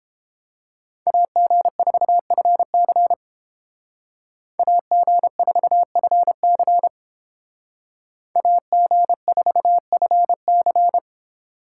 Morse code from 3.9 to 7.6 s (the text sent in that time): AG4FC